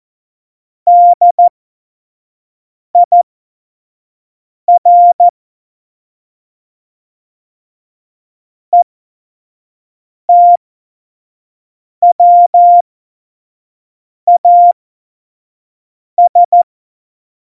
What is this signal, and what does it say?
Morse code, heard as a keyed tone, about 14 wpm, spaced as for 5 wpm, 700 Hz. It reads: DIR ETWAS
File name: Morsekurs / Witze / Witz 21. November 2017, 25 F70